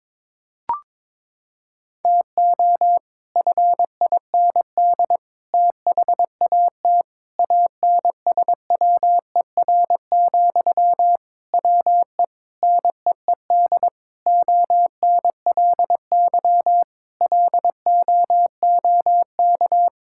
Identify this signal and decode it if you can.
Morse code, heard as a keyed tone, about 22 wpm, 700 Hz. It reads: TO FIND THAT ANSWER, WE NEED ONLY LOOK
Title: Sentences from Top 1600 Words 22wpm